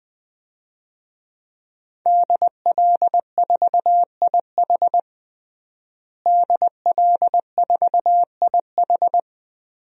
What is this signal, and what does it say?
Morse code, heard as a keyed tone, about 20 wpm, 700 Hz. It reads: DL4IH DL4IH
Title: DXCC-100 Call Signs - 5 Characters - Repeated 3x - 20wpm